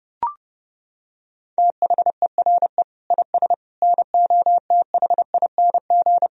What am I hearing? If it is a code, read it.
Morse code, heard as a keyed tone, about 32 wpm, 700 Hz. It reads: THERE IS NOTHING
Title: Sentences from Top 1000 Words - Review Part 3 - 30wpm